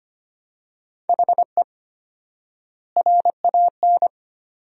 Morse code, heard as a keyed tone, about 25 wpm, 700 Hz. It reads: HE RAN
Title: 4x Word-Spacing - Sentences from Top 500 Words - 25wpm